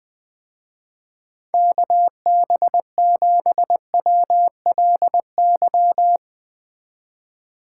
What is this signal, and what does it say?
Morse code, heard as a keyed tone, about 20 wpm, 700 Hz. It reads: KB7WLY
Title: Speed Racing - USA 2x3 Call Signs 17wpm